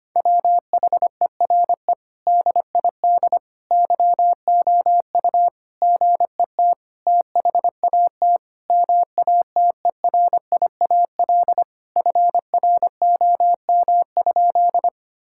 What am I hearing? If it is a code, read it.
Morse code, heard as a keyed tone, about 25 wpm, 700 Hz. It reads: WHERE DID YOU GET THAT MATERIAL FROM?
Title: Sentences from Top 1000 Words - Review Part 5 - 25wpm